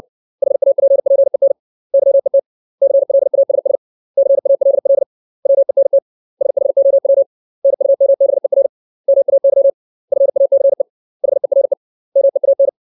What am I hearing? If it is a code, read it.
Morse code, heard as a keyed tone, about 60 wpm, 550 Hz. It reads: HAPPEN YET CLASS CALL WENT SHOW NUMBER MAY FACE HERE MEAN
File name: 632 words / 632 Words 60WPM